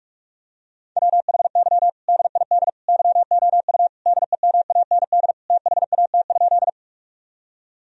Morse code, heard as a keyed tone, about 46 wpm, 700 Hz. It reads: WHY DID YOU DEMAND THAT?
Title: Sentences from Top 1000 Words - Review Part 4 - 45wpm